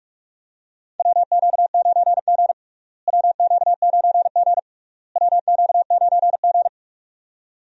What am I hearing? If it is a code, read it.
Morse code, heard as a keyed tone, about 41 wpm, 700 Hz. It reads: WQ9G WQ9G WQ9G
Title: USA Call Signs - 4 Characters - Repeated 3x - 45wpm